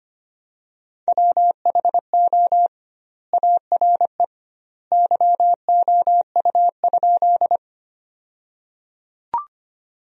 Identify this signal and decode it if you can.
Morse code, heard as a keyed tone, about 25 wpm, 700 Hz. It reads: WHO ARE YOU?
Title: 2x Word-Spacing - Sentences from Top 100 Words - 25wpm